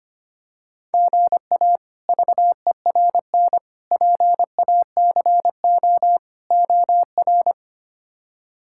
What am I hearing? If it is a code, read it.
Morse code, heard as a keyed tone, about 25 wpm, 700 Hz. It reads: GA VERN PACO OR
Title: K1USN SST QSO Practice - 25wpm